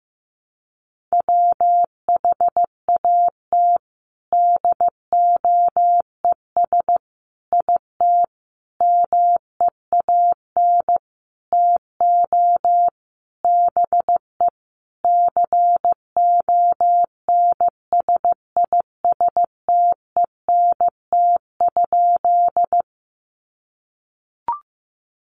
Morse code, heard as a keyed tone, about 15 wpm, 700 Hz. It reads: WHAT DOES IT MEAN TO BE CONSISTENT?
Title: Sentences from Top 1500 Words 15wpm